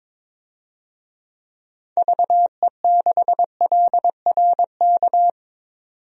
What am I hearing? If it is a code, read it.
Morse code, heard as a keyed tone, about 22 wpm, 700 Hz. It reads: VE6LRK
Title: DXCC-100 Call Signs - 6 Characters - 22wpm